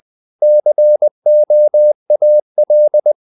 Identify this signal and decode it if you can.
Morse code, heard as a keyed tone, about 20 wpm, 600 Hz. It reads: COAL
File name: the old man and the sea part 2 20wpm